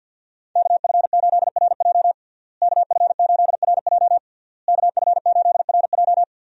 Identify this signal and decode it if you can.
Morse code, heard as a keyed tone, about 52 wpm, 700 Hz. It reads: KF8RJ KF8RJ KF8RJ
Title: USA Call Signs - 5 Characters - Repeated 3x - 50wpm